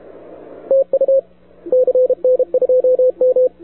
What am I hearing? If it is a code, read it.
Morse code, heard as a keyed tone, about 33 wpm, 470 Hz, with radio noise. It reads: TU CN2M
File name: wwcw08 / k5zd wwcw08 1700z